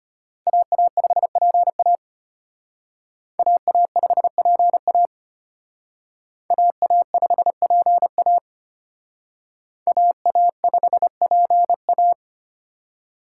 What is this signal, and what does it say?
Morse code, heard as a keyed tone, about 32 wpm, 700 Hz. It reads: AA5PA AA5PA AA5PA AA5PA